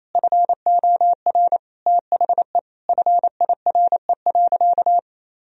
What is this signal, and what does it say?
Morse code, heard as a keyed tone, about 28 wpm, 700 Hz. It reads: FOR THE FIRE.
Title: Sentences from Top 1000 Words - Review Part 2 - 28wpm